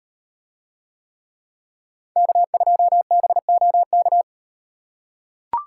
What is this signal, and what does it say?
Morse code, heard as a keyed tone, about 35 wpm, 700 Hz. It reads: K2BOK